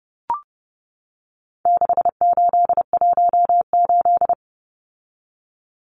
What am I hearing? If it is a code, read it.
Morse code, heard as a keyed tone, about 30 wpm, 700 Hz. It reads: 6818
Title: Numbers - 4-Digit 30wpm